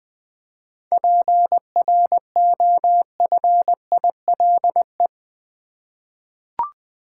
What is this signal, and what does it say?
Morse code, heard as a keyed tone, about 20 wpm, 700 Hz. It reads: PROFILE
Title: 7 Letter Words - First 3 Letters Given - 20wpm